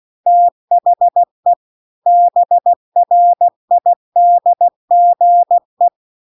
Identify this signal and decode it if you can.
Morse code, heard as a keyed tone, about 16 wpm, 700 Hz. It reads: THE BRIDGE
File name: CW-The-Big-Twoheaded-River-1-16WPM